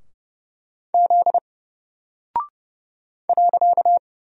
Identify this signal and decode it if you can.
Morse code, heard as a keyed tone, about 31 wpm, 700 Hz. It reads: Z .
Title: Single Character (Letter:Number:Punctuation:Prosign) 30wpm